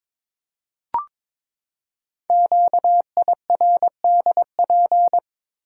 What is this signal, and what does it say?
Morse code, heard as a keyed tone, about 22 wpm, 700 Hz. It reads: QIRDP
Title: Code Groups - 5 Letter:Numbers - 22wpm